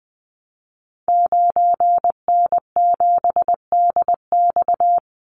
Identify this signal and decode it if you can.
Morse code, heard as a keyed tone, about 20 wpm, 700 Hz. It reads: 9N7DX
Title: Call Signs Repeated 3x - 20wpm